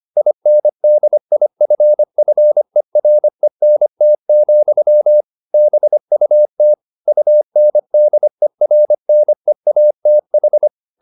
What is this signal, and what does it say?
Morse code, heard as a keyed tone, about 25 wpm, 600 Hz. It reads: INDIFFERENT, BUT UNDERNEATH